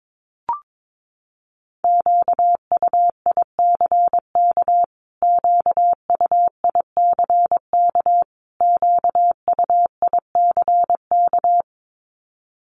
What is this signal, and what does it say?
Morse code, heard as a keyed tone, about 22 wpm, 700 Hz. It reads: QUICK QUICK QUICK
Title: Top 500 Words Repeated 22wpm